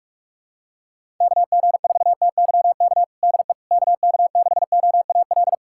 Morse code, heard as a keyed tone, about 46 wpm, 700 Hz. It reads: KG4TYK DE KK6OAL
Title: USA Callsign DE Callsign - 6 Characters - 45wpm